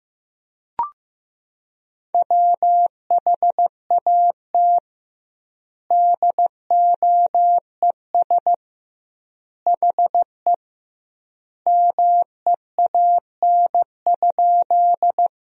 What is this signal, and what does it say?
Morse code, heard as a keyed tone, about 15 wpm, 700 Hz. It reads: WHAT DOES HE MEAN?
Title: 2x Word-Spacing - Sentences from Top 200 Words - 15wpm